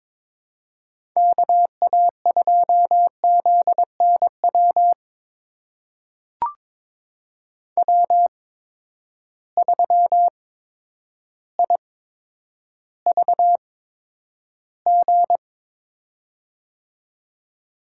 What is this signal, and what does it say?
Morse code, heard as a keyed tone, about 21 wpm, spaced as for 10 wpm, 700 Hz. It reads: KA2ZNW W3IVG
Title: USA Call Signs - 8x Character Spacing - 22wpm